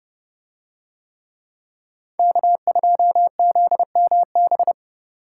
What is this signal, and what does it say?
Morse code, heard as a keyed tone, about 27 wpm, 700 Hz. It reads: K2ZMB